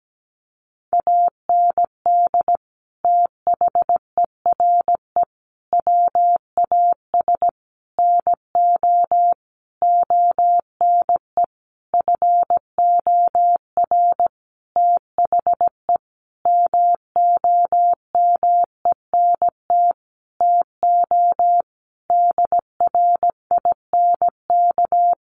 Morse code, heard as a keyed tone, about 17 wpm, 700 Hz. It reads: AND THERE WAS NO ONE FOR THE MOMENT TO DRINK
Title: Sentences from Top 1500 Words 17wpm